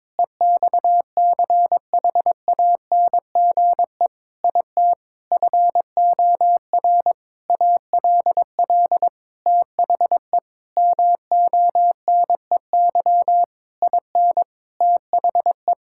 Morse code, heard as a keyed tone, about 22 wpm, 700 Hz. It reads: EXCHANGE IT FOR ALL THE MONEY IN THE
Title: Sentences from Top 1000 Words - Review Part 6 - 22wpm